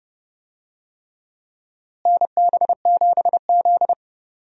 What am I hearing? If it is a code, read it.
Morse code, heard as a keyed tone, about 30 wpm, 700 Hz. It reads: NB7Z
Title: USA Call Signs - 4 Characters - Repeated 3x - 30wpm